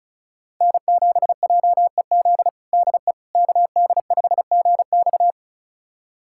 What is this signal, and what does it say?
Morse code, heard as a keyed tone, about 36 wpm, 700 Hz. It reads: N7JEZ DE KD5GX
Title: USA Callsign DE Callsign - 5 Characters - 35wpm